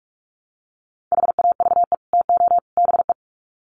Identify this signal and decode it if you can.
Morse code, heard as a keyed tone, about 44 wpm, 700 Hz. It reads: HAVE TO BE
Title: Sets of 3 Words 45wpm